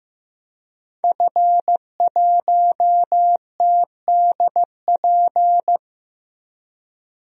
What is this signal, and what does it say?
Morse code, heard as a keyed tone, about 16 wpm, 700 Hz. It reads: F1TDP
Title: DXCC-100 Call Signs - 5 Characters - Repeated 2x - 15wpm